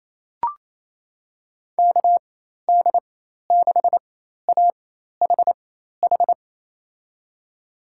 Morse code, heard as a keyed tone, about 28 wpm, spaced as for 13 wpm, 700 Hz. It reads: KD6AHH
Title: USA Call Signs - 6 Characters - 4x Character Spacing - 28wpm